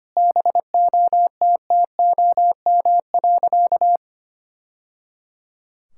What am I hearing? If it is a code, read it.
Morse code, heard as a keyed tone, about 25 wpm, 700 Hz. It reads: BOTTOM.
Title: Sentences from Top 1000 Words - Review Part 6 - 25wpm